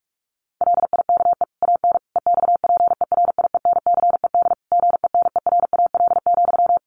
Morse code, heard as a keyed tone, about 44 wpm, 700 Hz. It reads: LIKE AN EXPERIENCED GENERAL,
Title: Sentences from Top 1500 Words 45wpm